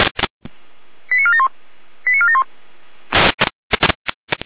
Five-tone selective call (ZVEI), heard as quick runs of five tones, tones of about 70 ms. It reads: 89471 89471